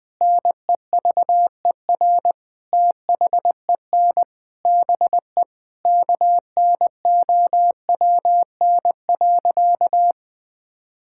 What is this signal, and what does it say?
Morse code, heard as a keyed tone, about 20 wpm, 700 Hz. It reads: NEVER THEN BE KNOWN.
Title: Sentences from Top 1600 Words 20wpm